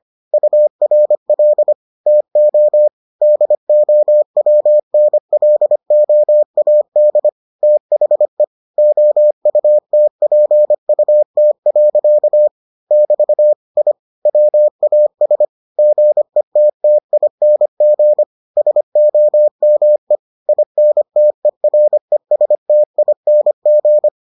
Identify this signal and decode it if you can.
Morse code, heard as a keyed tone, about 25 wpm, 600 Hz. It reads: URL TO DOWNLOAD THE OUTPUT. <BT> I WAS GETTING SOME INTERESTING